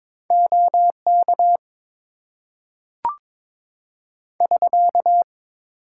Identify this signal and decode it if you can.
Morse code, heard as a keyed tone, about 22 wpm, 700 Hz. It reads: OK <SK>
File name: Top 90 QSO Elements - 8x Character Spacing - 22wpm